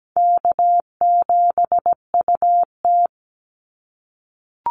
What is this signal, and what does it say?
Morse code, heard as a keyed tone, about 17 wpm, 700 Hz. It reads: K7UT
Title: Easy Call Signs - 17wpm